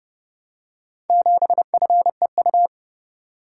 Morse code, heard as a keyed tone, about 30 wpm, 700 Hz. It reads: 7FEU